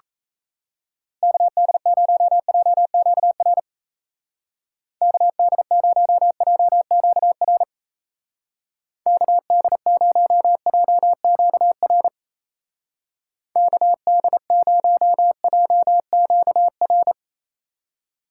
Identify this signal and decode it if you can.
Morse code, heard as a keyed tone, about 35 wpm, 700 Hz. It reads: KD0JQR KD0JQR KD0JQR KD0JQR
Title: Speed Racing - USA 2x3 Call Signs 28wpm